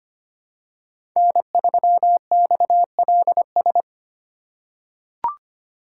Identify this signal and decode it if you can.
Morse code, heard as a keyed tone, about 25 wpm, 700 Hz. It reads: N3XLS
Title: K1USN SST QSO Practice - 25wpm